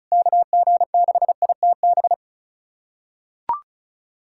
Morse code, heard as a keyed tone, about 36 wpm, 700 Hz. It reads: KG6ITB
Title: USA Callsign DE Callsign 35wpm